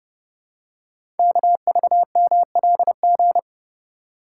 Morse code, heard as a keyed tone, about 30 wpm, 700 Hz. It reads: KVMLG
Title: Code Groups - 5 Letters - 30wpm